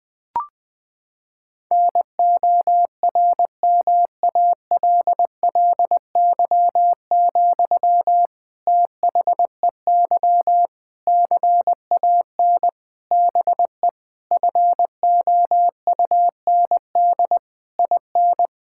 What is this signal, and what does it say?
Morse code, heard as a keyed tone, about 20 wpm, 700 Hz. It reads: NORMALLY, THEY CAN BE FOUND IN
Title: Sentences from Top 1000 Words - Review Part 6 - 20wpm